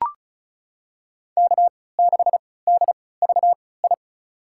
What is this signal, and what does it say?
Morse code, heard as a keyed tone, about 36 wpm, spaced as for 20 wpm, 700 Hz. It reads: K6DVI
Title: USA Call Signs - 5 Characters - 3x Character Spacing - 35wpm